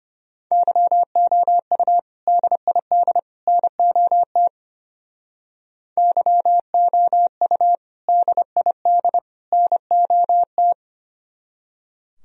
Morse code, heard as a keyed tone, about 32 wpm, 700 Hz. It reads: YOU DID NOT YOU DID NOT